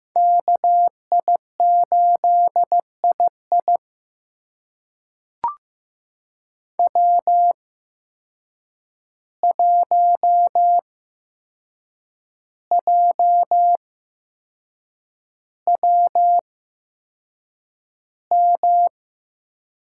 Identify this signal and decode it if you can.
Morse code, heard as a keyed tone, about 15 wpm, spaced as for 6 wpm, 700 Hz. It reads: KI8II W1JWM
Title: USA Call Signs - 8x Character Spacing - 15wpm